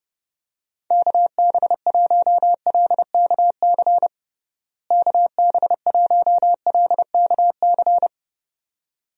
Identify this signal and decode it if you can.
Morse code, heard as a keyed tone, about 30 wpm, 700 Hz. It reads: KB1LKC KB1LKC
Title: USA Call Signs - 6 Characters - Repeated 2x - 30wpm